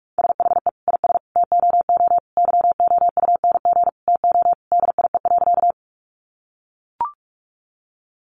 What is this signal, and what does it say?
Morse code, heard as a keyed tone, about 44 wpm, 700 Hz. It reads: SHE IS TOO YOUNG TO DIE.
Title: Sentences from Top 1300 Words 45wpm